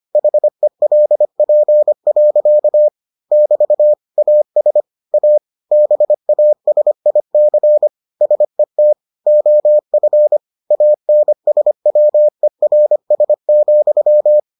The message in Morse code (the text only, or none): HELP. <BT> AS A BASIC SET OF ANSWERS,